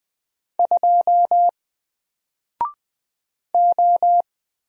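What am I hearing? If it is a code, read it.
Morse code, heard as a keyed tone, about 20 wpm, 700 Hz. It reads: 2 O
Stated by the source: Single letter and number Morse drill at 20 wpm, from N4LCD.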